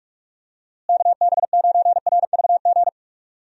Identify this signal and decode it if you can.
Morse code, heard as a keyed tone, about 43 wpm, 700 Hz. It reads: KB9RVG
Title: USA Call Signs - 6 Characters - Repeated 2x - 45wpm